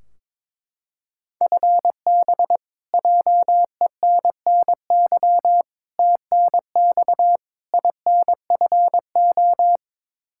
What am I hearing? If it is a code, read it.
Morse code, heard as a keyed tone, about 22 wpm, 700 Hz. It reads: FB JENNY TNX INFO
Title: Practice QSO - Exchange 3 - More Info - 22wpm